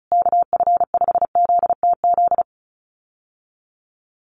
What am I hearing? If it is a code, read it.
Morse code, heard as a keyed tone, about 34 wpm, 700 Hz. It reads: KF5ZTZ
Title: USA Call Signs Repeated 3x 35wpm